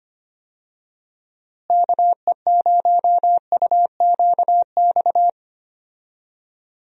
Morse code, heard as a keyed tone, about 25 wpm, 700 Hz. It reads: KE0UQX
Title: USA Call Signs - 6 Characters - Repeated 3x - 25wpm